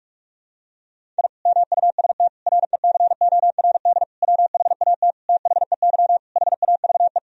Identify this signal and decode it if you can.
Morse code, heard as a keyed tone, about 46 wpm, 700 Hz. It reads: I MUST RECORD WHAT THEY HAVE
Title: Sentences from Top 1000 Words - Review Part 3 - 45wpm